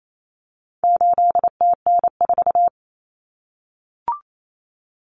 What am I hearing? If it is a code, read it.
Morse code, heard as a keyed tone, about 28 wpm, 700 Hz. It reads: 8TN4
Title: Common Cut Numbers - 4-Digit 28wpm